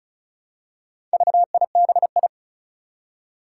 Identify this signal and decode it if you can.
Morse code, heard as a keyed tone, about 36 wpm, 700 Hz. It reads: VIBI
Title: Four Challenging Letter-Number Combinations 35wpm